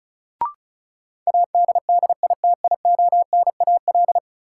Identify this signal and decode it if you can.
Morse code, heard as a keyed tone, about 36 wpm, 700 Hz. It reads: ADDITIONAL
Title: Sentences from Top 1400 Words - New Word Drill Alt - 35wpm